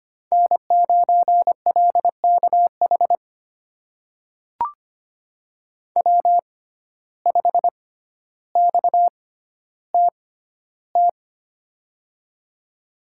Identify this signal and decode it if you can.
Morse code, heard as a keyed tone, about 25 wpm, spaced as for 13 wpm, 700 Hz. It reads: N9LKH W5XTT
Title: USA Call Signs - 5 Characters - 6x Character Spacing - 25wpm